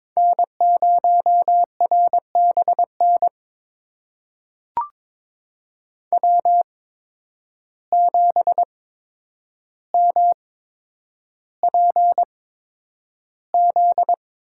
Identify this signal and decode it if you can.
Morse code, heard as a keyed tone, about 21 wpm, spaced as for 9 wpm, 700 Hz. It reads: N0RBN W7MPZ
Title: USA Call Signs - 5 Characters - 8x Character Spacing - 22wpm